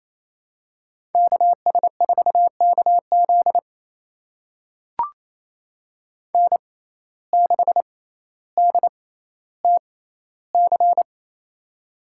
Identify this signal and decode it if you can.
Morse code, heard as a keyed tone, about 28 wpm, spaced as for 14 wpm, 700 Hz. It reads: KS4KZ N6DTC
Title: USA Call Signs - 6x Character Spacing - 28wpm